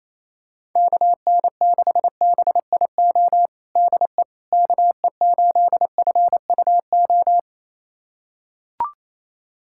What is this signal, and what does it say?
Morse code, heard as a keyed tone, about 28 wpm, 700 Hz. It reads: KN6BIO DE KE8FUO
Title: USA Callsign DE Callsign - 6 Characters - 28wpm